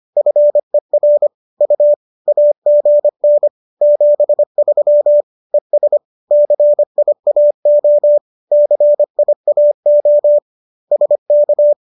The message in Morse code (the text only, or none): FER U AGN 73 ES CIAO CIAO SK